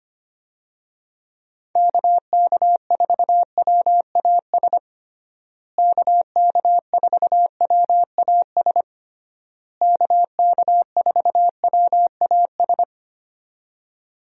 Morse code, heard as a keyed tone, about 25 wpm, 700 Hz. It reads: KK4WAS KK4WAS KK4WAS
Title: USA Call Signs - 6 Characters - Repeated 3x - 25wpm